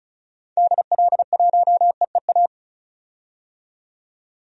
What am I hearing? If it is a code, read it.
Morse code, heard as a keyed tone, about 34 wpm, 700 Hz. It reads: DL1EEA
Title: DXCC-100 Call Signs - 6 Characters - 35wpm